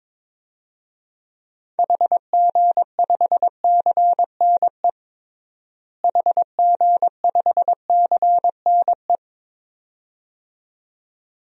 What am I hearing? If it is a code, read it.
Morse code, heard as a keyed tone, about 22 wpm, 700 Hz. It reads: HG5CNE HG5CNE
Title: DXCC-100 Call Signs - 6 Characters - Repeated 2x - 22wpm